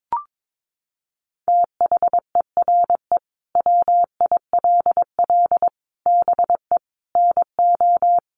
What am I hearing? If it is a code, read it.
Morse code, heard as a keyed tone, about 22 wpm, 700 Hz. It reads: THERE WILL BE NO
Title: Sentences from Top 1500 Words 22wpm